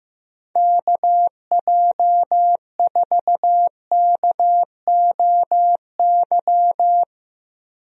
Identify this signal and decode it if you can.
Morse code, heard as a keyed tone, about 15 wpm, 700 Hz. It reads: KJ4KOY
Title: USA Call Signs - 6 Characters - Repeated 3x - 15wpm